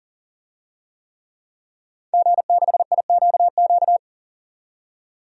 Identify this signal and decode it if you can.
Morse code, heard as a keyed tone, about 40 wpm, 700 Hz. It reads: G6IQQ
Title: DXCC-100 Call Signs - 5 Characters - 40wpm